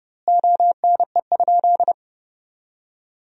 Morse code, heard as a keyed tone, about 30 wpm, 700 Hz. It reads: ONE?
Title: 4x Word-Spacing - Sentences from Top 300 Words - 30wpm